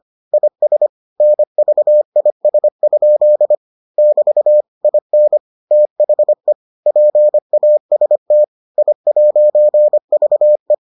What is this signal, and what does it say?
Morse code, heard as a keyed tone, about 25 wpm, 600 Hz. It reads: IS NVIS? <BT> IN THE PAST I'VE